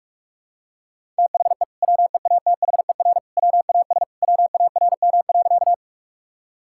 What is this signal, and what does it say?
Morse code, heard as a keyed tone, about 45 wpm, 700 Hz. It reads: THE WEATHER WAS WARM.
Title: Sentences from Top 1000 Words - Review Part 5 - 45wpm